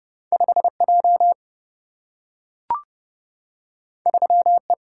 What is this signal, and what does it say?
Morse code, heard as a keyed tone, about 30 wpm, 700 Hz. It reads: 5J 3E
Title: Single-Letter Single-Number 30wpm